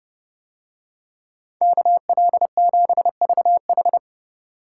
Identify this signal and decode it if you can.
Morse code, heard as a keyed tone, about 28 wpm, 700 Hz. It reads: KL7VH